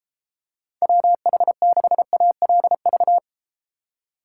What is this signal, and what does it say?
Morse code, heard as a keyed tone, about 32 wpm, 700 Hz. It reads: WH6ALV